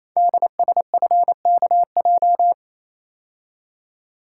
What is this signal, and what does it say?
Morse code, heard as a keyed tone, about 29 wpm, 700 Hz. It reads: DSFKJ
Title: Code Groups - 5 Letters - 28wpm